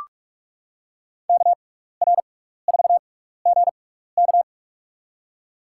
Morse code, heard as a keyed tone, about 44 wpm, spaced as for 15 wpm, 700 Hz. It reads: KR4GK
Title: USA Call Signs - 5 Characters - 6x Character Spacing - 45wpm